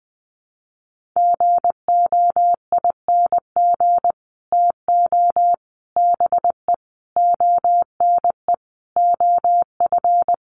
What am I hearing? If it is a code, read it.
Morse code, heard as a keyed tone, about 20 wpm, 700 Hz. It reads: GOING TO BE ONE OF
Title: Sets of 5 Words Encore2 20wpm